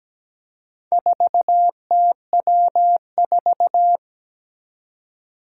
Morse code, heard as a keyed tone, about 17 wpm, 700 Hz. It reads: 4TW4
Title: Four Challenging Letter-Number Combinations 17wpm